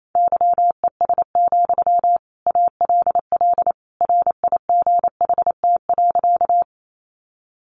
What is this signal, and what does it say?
Morse code, heard as a keyed tone, about 28 wpm, 700 Hz. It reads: YES, ALL RIGHT.
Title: Sentences from Top 500 Words 28wpm